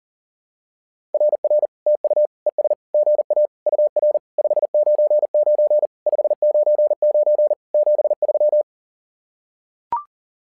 Morse code, heard as a keyed tone, about 40 wpm, 600 Hz. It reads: RR TU ES GA UR 599 599 73
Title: SOTA QSO Practice 40wpm